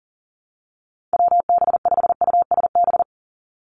Morse code, heard as a keyed tone, about 40 wpm, 700 Hz. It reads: WB5USB
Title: NA SOTA Chaser Callsigns 40wpm